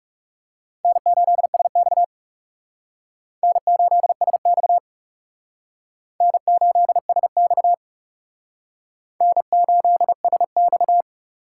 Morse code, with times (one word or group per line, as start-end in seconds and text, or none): 0.8-2.0 N8SX
3.4-4.8 N8SX
6.2-7.7 N8SX
9.2-11.0 N8SX